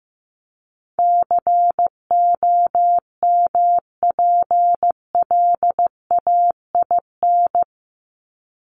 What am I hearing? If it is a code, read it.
Morse code, heard as a keyed tone, about 14 wpm, 700 Hz. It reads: COMPLAIN
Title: Top 1001-1250 Words 15wpm